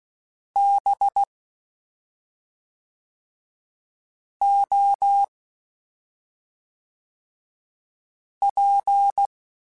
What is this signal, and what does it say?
Morse code, heard as a keyed tone, about 16 wpm, 800 Hz. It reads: B O P